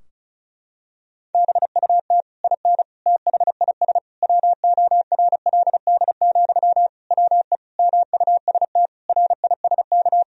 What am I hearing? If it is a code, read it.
Morse code, heard as a keyed tone, about 36 wpm, 700 Hz. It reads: BUT IN THIS WORLD, WE MUST RISK
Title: Sentences from Top 1000 Words - Review Part 4 - 35wpm